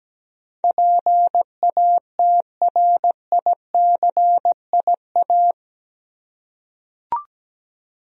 Morse code, heard as a keyed tone, about 17 wpm, 700 Hz. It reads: PATRICIA
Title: USA Common Names - 17wpm